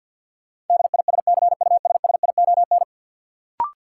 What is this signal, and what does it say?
Morse code, heard as a keyed tone, about 46 wpm, 700 Hz. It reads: DISCUSSION
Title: Sentences from Top 1100 Words - New Word Drill - 50wpm